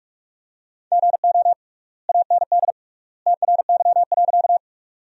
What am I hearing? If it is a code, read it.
Morse code, heard as a keyed tone, about 46 wpm, 700 Hz. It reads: GO AND TRY.
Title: 3x Word-Spacing - Sentences from Top 200 Words - 45wpm